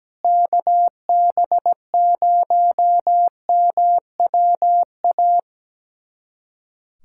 Morse code, heard as a keyed tone, about 17 wpm, 700 Hz. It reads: KB0MWA
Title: USA Callsign DE Callsign - 6 Characters - 17wpm